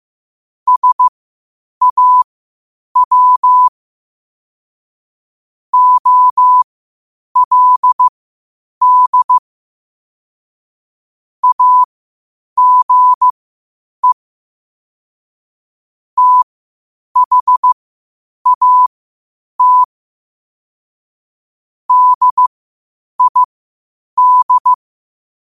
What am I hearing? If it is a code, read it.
Morse code, heard as a keyed tone, about 15 wpm, spaced as for 8 wpm, 1000 Hz. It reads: SAW OLD AGE THAT DID
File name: qotdfiles / 8wpm